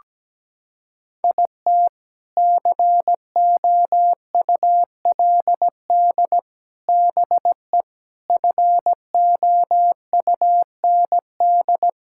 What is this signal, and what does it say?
Morse code, heard as a keyed tone, about 17 wpm, 700 Hz. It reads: IT COULD BE FOUND
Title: Sentences from Top 1000 Words - Review Part 5 - 17wpm